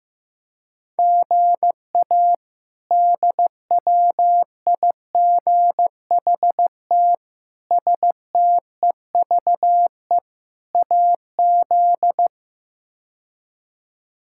Morse code, heard as a keyed tone, about 15 wpm, 700 Hz. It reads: GA DWIGHT STEVE AZ
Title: K1USN SST QSO Practice - 15wpm